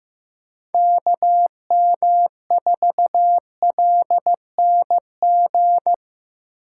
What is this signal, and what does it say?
Morse code, heard as a keyed tone, about 15 wpm, 700 Hz. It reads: KM4LNG
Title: USA Call Signs - 6 Characters - 4x Character Spacing - 15wpm